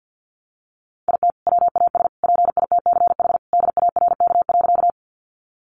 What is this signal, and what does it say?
Morse code, heard as a keyed tone, about 50 wpm, 700 Hz. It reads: IT WAS PITCH DARK.
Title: Sentences from Top 1300 Words 50wpm